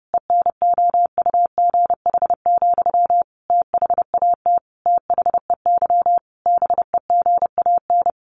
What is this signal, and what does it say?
Morse code, heard as a keyed tone, about 30 wpm, 700 Hz. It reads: ENOUGH, THAT THEY BEGAN
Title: Sentences from Top 800 Words 30wpm